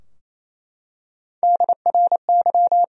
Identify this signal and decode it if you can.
Morse code, heard as a keyed tone, about 28 wpm, 700 Hz. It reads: DRY
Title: Sentences from Top 1000 Words - Review Part 3 - 28wpm